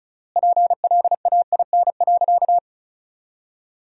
Morse code, heard as a keyed tone, about 36 wpm, 700 Hz. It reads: PLAIN.
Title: 3x Word-Spacing - Sentences from Top 400 Words - 35wpm